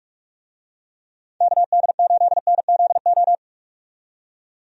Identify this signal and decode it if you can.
Morse code, heard as a keyed tone, about 46 wpm, 700 Hz. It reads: KD8NZO
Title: USA Call Signs - 6 Characters - 45wpm